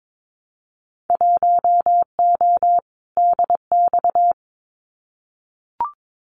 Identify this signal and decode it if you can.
Morse code, heard as a keyed tone, about 22 wpm, 700 Hz. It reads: 1O DX
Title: Winter Field Day Exchanges 22wpm